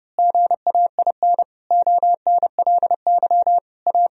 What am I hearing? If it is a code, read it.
Morse code, heard as a keyed tone, about 30 wpm, 700 Hz. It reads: GAIN ONLY A
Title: Sentences from Top 1000 Words - Review Part 2 - 30wpm